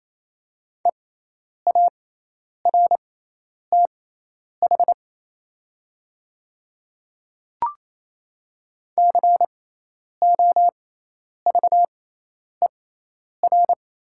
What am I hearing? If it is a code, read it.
Morse code, heard as a keyed tone, about 28 wpm, spaced as for 10 wpm, 700 Hz. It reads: EARTH COVER
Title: 6x Character Spacing - Top 300 Words - 28wpm